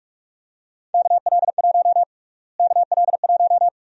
Morse code, heard as a keyed tone, about 46 wpm, 700 Hz. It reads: KL1 KL1
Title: USA Call Sign Prefixes - Repeated 2x - 45wpm